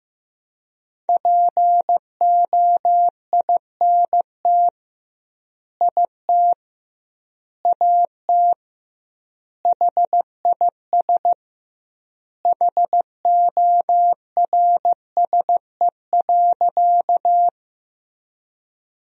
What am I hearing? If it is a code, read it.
Morse code, heard as a keyed tone, about 15 wpm, 700 Hz. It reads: POINT IT AT HIS HORSE.
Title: 2x Word-Spacing - Sentences from Top 300 Words - 15wpm